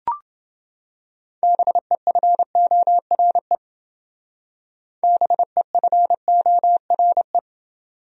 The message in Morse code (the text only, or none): BEFORE BEFORE